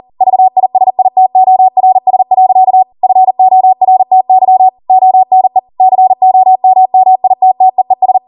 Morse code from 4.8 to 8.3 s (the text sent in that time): ONE COMMITTEES